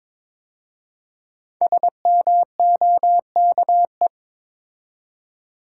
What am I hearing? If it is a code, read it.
Morse code, heard as a keyed tone, about 22 wpm, 700 Hz. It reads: SMOKE
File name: 5 Letter Words - First 3 Letters Given - 22wpm